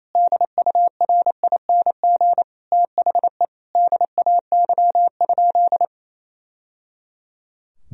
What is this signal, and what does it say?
Morse code, heard as a keyed tone, about 28 wpm, 700 Hz. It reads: DURING THE DAY?
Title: Sentences from Top 1000 Words - Review Part 2 - 28wpm